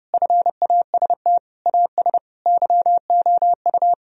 Morse code, heard as a keyed tone, about 30 wpm, 700 Hz. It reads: FAST AS YOU
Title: Sentences from Top 1000 Words - Review Part 2 - 30wpm